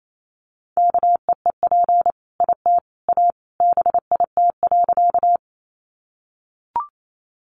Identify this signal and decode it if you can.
Morse code, heard as a keyed tone, about 28 wpm, 700 Hz. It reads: KEEP IT A BIT.
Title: Sentences from Top 900 Words 28wpm